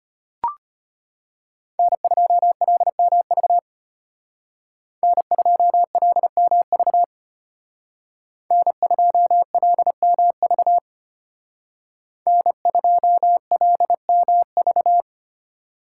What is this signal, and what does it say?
Morse code, heard as a keyed tone, about 33 wpm, 700 Hz. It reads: N2LMV N2LMV N2LMV N2LMV